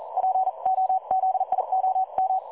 Morse code, heard as a keyed tone, about 42 wpm, spaced as for 36 wpm, 750 Hz. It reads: POZIOM
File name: dlazaawansowanych / nagraniazpasma / websdr recording 3565.0kHz